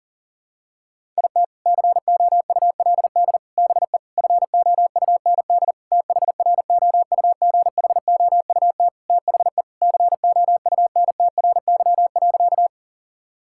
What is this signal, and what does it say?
Morse code, heard as a keyed tone, about 40 wpm, 700 Hz. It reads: IT COULD BE FOUND THROUGHOUT THE COUNTRY.
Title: Sentences from Top 1000 Words - Review Part 5 - 40wpm